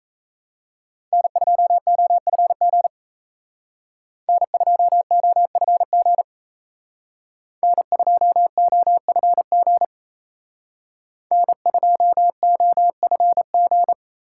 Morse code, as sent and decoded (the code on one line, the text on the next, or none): -. ..--- --- ..-. --. / -. ..--- --- ..-. --. / -. ..--- --- ..-. --. / -. ..--- --- ..-. --.
N2OFG N2OFG N2OFG N2OFG